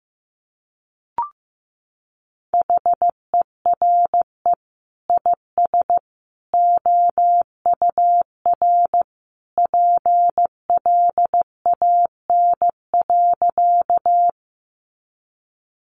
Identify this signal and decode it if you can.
Morse code, heard as a keyed tone, about 15 wpm, 700 Hz. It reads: HERE IS OUR PLAN.
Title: Sentences from Top 500 Words 15wpm